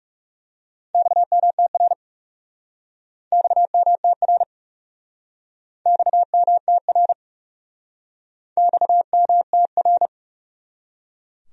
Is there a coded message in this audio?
Yes: Morse code from 0.9 to 10.1 s, heard as a keyed tone, about 40 wpm, 700 Hz. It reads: XMTR XMTR XMTR XMTR